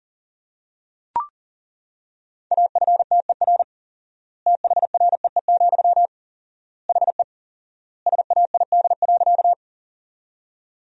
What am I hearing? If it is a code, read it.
Morse code, heard as a keyed tone, about 40 wpm, 700 Hz. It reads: AFTER THREE, HE SAID.
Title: 4x Word-Spacing - Sentences from Top 200 Words - 40wpm